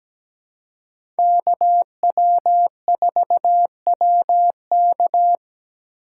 Morse code, heard as a keyed tone, about 17 wpm, 700 Hz. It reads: KW4WK
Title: USA Call Signs - 5 Characters - 2x Character Spacing - 17wpm